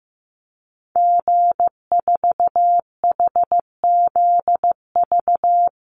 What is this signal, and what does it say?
Morse code, heard as a keyed tone, about 15 wpm, 700 Hz. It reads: G4HZV
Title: Call Signs Repeated 3x - 15wpm